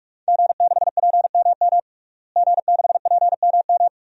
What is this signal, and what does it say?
Morse code, heard as a keyed tone, about 46 wpm, 700 Hz. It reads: G6PMM G6PMM
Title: DXCC-100 Call Signs - 5 Characters - Repeated 2x - 45wpm